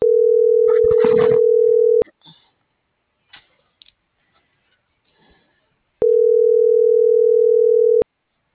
Ringback tone (North America), 440 + 480 Hz.